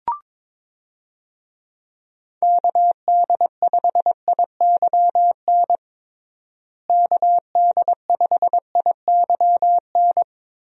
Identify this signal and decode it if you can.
Morse code, heard as a keyed tone, about 22 wpm, 700 Hz. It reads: KD5IYN KD5IYN
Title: USA Call Signs - 6 Characters - Repeated 3x - 22wpm